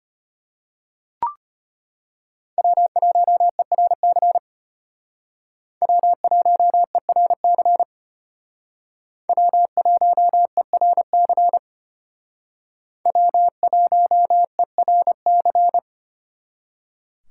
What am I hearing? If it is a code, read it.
Morse code, heard as a keyed tone, about 32 wpm, 700 Hz. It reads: W1ERC W1ERC W1ERC W1ERC